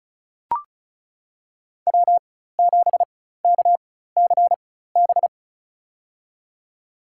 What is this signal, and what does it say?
Morse code, heard as a keyed tone, about 36 wpm, spaced as for 16 wpm, 700 Hz. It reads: W7KCB
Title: USA Call Signs - 5 Characters - 4x Character Spacing - 35wpm